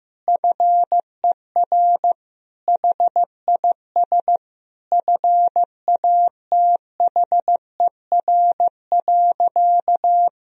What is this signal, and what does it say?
Morse code, heard as a keyed tone, about 16 wpm, 700 Hz. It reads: FER HIS FATHER.
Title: Sentences from Top 500 Words and Abbreviations 15wpm